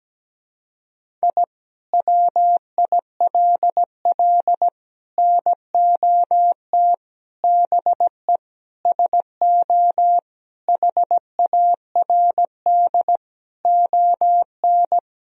Morse code, heard as a keyed tone, about 17 wpm, 700 Hz. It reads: I WILL NOT BE SO HARD ON
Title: Sentences from Top 1000 Words - Review Part 1 - 17wpm